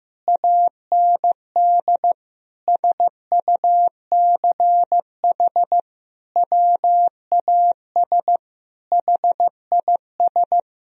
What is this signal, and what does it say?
Morse code, heard as a keyed tone, about 15 wpm, 700 Hz. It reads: AND SUCH WAS HIS
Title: Sentences from Top 1000 Words - Review Part 4 - 15wpm